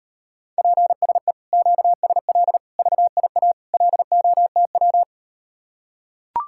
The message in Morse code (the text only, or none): PSE QSL VIA LOTW